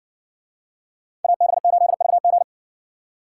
Morse code, heard as a keyed tone, about 57 wpm, 700 Hz. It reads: AB8VG